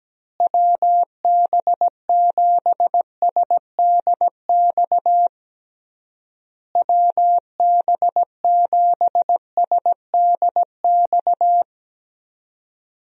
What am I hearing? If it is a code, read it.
Morse code, heard as a keyed tone, about 17 wpm, 700 Hz. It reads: WB7SDX WB7SDX
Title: USA Call Signs - 6 Characters - Repeated 3x - 17wpm